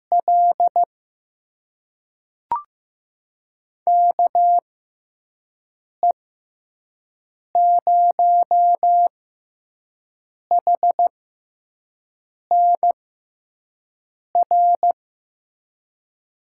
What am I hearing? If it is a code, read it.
Morse code, heard as a keyed tone, about 15 wpm, spaced as for 5 wpm, 700 Hz. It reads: L KE0HNR
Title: USA Call Signs - 6 Characters - 6x Character Spacing - 15wpm